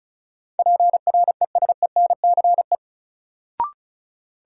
Morse code, heard as a keyed tone, about 36 wpm, 700 Hz. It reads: PRESENCE
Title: Sets of 3 Words - New Word Drill - 35wpm